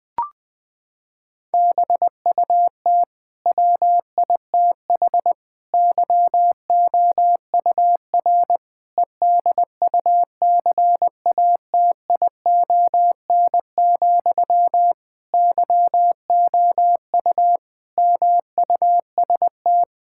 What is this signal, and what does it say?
Morse code, heard as a keyed tone, about 21 wpm, 700 Hz. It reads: BUT WITH YOUR EDUCATION, YOU MUST
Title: Sentences from Top 1000 Words - Review Part 6 - 20wpm